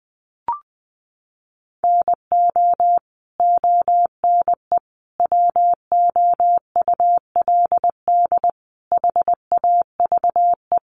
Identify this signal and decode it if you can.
Morse code, heard as a keyed tone, about 20 wpm, 700 Hz. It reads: NO ONE WOULD HAVE
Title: Sentences from Top 300 Words 20wpm